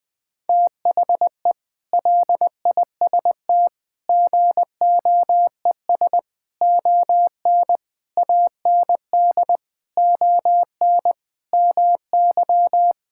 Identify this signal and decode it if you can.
Morse code, heard as a keyed tone, about 20 wpm, 700 Hz. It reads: THE LIST GOES ON AND ON MY
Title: Sentences with 6-Word Phrases - 20wpm